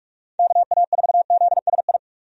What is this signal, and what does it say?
Morse code, heard as a keyed tone, about 47 wpm, 700 Hz. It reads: KA4ZSI
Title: USA Call Signs - 6 Characters - 45wpm